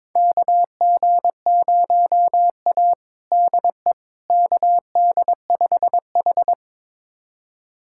Morse code, heard as a keyed tone, about 23 wpm, 700 Hz. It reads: KG0A DE KD5H
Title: USA Callsign DE Callsign - 4 Characters - 22wpm